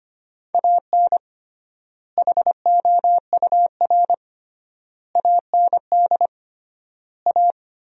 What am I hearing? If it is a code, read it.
Morse code, heard as a keyed tone, about 25 wpm, 700 Hz. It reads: AN HOUR AND A
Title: 3x Word-Spacing - Sentences from Top 400 Words - 25wpm